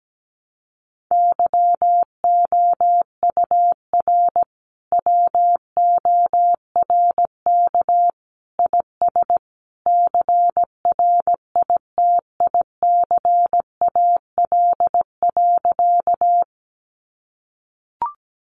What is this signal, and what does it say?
Morse code, heard as a keyed tone, about 17 wpm, 700 Hz. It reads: YOUR WORK IS CRITICAL.
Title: Sentences from Top 1100 Words 17wpm